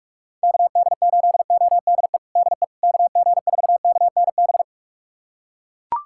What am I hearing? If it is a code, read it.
Morse code, heard as a keyed tone, about 41 wpm, 690 Hz. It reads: KD8ODE DE KG4KNB
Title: USA Callsign DE Callsign - 6 Characters - 45wpm